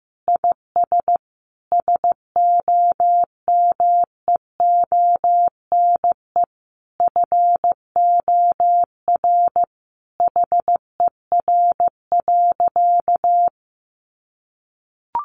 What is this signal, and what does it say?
Morse code, heard as a keyed tone, about 15 wpm, 700 Hz. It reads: IS SOMEONE FOR HER.
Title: Sentences from Top 600 Words 15wpm